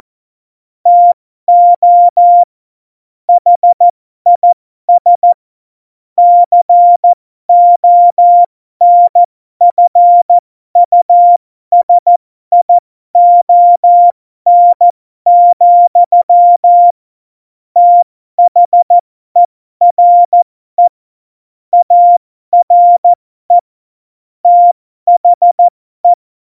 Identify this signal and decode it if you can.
Morse code, heard as a keyed tone, about 14 wpm, spaced as for 12 wpm, 700 Hz. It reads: TO HIS CONFUSION, THERE ARE THE